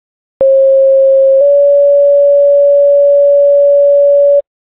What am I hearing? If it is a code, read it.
Two-tone page, heard as a one-second tone, then a three-second tone, about 540 then 570 Hz. It is heard from 0.4 to 4.4 s.